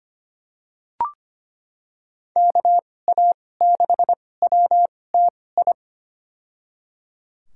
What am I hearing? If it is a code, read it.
Morse code, heard as a keyed tone, about 25 wpm, spaced as for 18 wpm, 700 Hz. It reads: KA6WTI
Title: USA Call Signs - 6 Characters - 2x Character Spacing - 25wpm